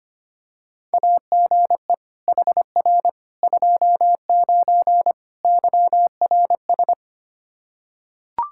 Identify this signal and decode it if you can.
Morse code, heard as a keyed tone, about 25 wpm, 700 Hz. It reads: AGE HR 29 YRS
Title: Practice QSO - Exchange 3 - More Info - 25wpm